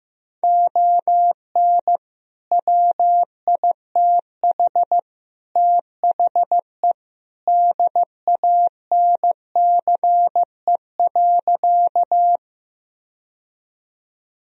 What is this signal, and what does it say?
Morse code, heard as a keyed tone, about 15 wpm, 700 Hz. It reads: ON WITH THE DANCE.
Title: Sentences from Top 1000 Words - Review Part 4 - 15wpm